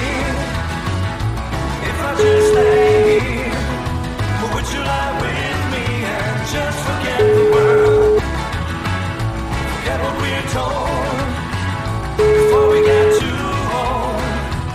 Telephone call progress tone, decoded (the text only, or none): ringback tone